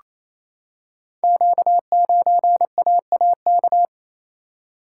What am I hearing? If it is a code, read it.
Morse code, heard as a keyed tone, about 27 wpm, 700 Hz. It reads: Q9AAK
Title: Code Groups - 5 Letter:Numbers - 28wpm